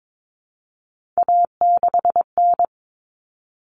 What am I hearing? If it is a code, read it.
Morse code, heard as a keyed tone, about 22 wpm, 700 Hz. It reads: A6N
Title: All Cut Numbers - 3-Digit 22wpm